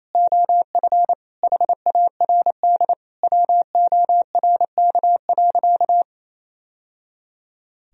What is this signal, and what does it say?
Morse code, heard as a keyed tone, about 28 wpm, 700 Hz. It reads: OF HARD WORK.
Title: Sentences from Top 1000 Words - Review Part 2 - 28wpm